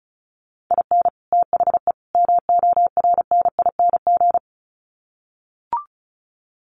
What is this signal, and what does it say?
Morse code, heard as a keyed tone, about 34 wpm, 700 Hz. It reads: IN THE MORNING
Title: Sets of 3 Words 35wpm